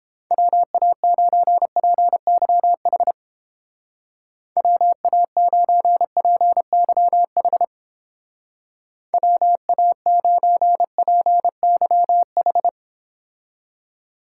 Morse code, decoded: WA9PYH WA9PYH WA9PYH

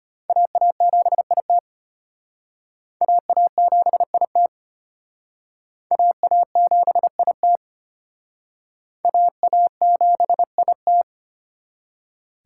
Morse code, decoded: AA7IT AA7IT AA7IT AA7IT

